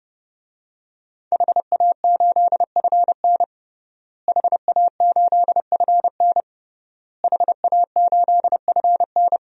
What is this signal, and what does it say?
Morse code, heard as a keyed tone, about 30 wpm, 700 Hz. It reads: HA8FN HA8FN HA8FN
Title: DXCC-100 Call Signs - 5 Characters - Repeated 3x - 30wpm